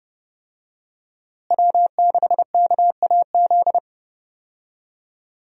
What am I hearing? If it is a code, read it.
Morse code, heard as a keyed tone, about 30 wpm, 700 Hz. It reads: W6KAZ
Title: USA Call Signs - 5 Characters - Repeated 3x - 30wpm